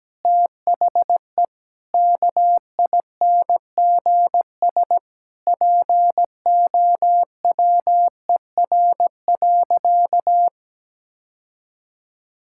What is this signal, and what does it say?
Morse code, heard as a keyed tone, about 17 wpm, 700 Hz. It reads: THE KINGS POWER.
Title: Sentences from Top 1000 Words - Review Part 6 - 17wpm